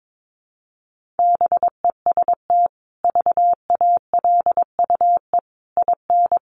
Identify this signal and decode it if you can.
Morse code, heard as a keyed tone, about 22 wpm, 700 Hz. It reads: BEST VALUE IN
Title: Sentences from Top 600 Words 22wpm